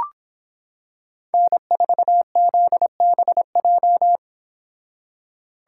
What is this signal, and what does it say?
Morse code, heard as a keyed tone, about 25 wpm, 700 Hz. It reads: N4ZBJ